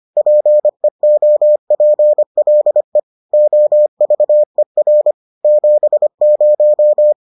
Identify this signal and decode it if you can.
Morse code, heard as a keyed tone, about 25 wpm, 600 Hz. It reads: PEOPLE OVER 70